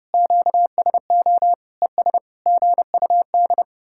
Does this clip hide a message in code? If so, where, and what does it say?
Yes: Morse code from 0.1 to 3.6 s, heard as a keyed tone, about 38 wpm, 700 Hz. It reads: QSO ES GUD